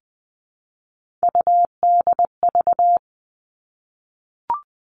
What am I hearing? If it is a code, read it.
Morse code, heard as a keyed tone, about 20 wpm, 700 Hz. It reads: UDV
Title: All Cut Numbers - 3-Digit 20wpm